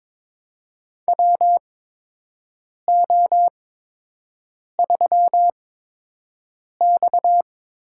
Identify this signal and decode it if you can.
Morse code, heard as a keyed tone, about 23 wpm, spaced as for 6 wpm, 700 Hz. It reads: WO3X
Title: USA Call Signs - 4 Characters - 8x Character Spacing - 22wpm